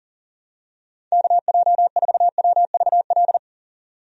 Morse code, heard as a keyed tone, about 40 wpm, 700 Hz. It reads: KJ4WVL
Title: USA Call Signs - 6 Characters - 40wpm